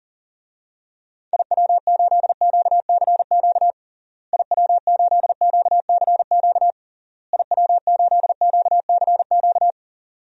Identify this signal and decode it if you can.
Morse code, heard as a keyed tone, about 40 wpm, 700 Hz. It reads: IW8QCQ IW8QCQ IW8QCQ
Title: DXCC-100 Call Signs - 6 Characters - Repeated 3x - 40wpm